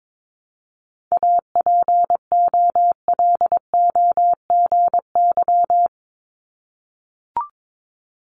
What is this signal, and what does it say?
Morse code, heard as a keyed tone, about 22 wpm, 700 Hz. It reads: APOLOGY
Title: Multi-letter Phonemes Repeated 3x 22wpm